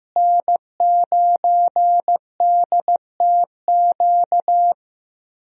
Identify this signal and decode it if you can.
Morse code, heard as a keyed tone, about 15 wpm, 700 Hz. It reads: N9DTQ
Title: USA Call Signs - 5 Characters - Repeated 3x - 15wpm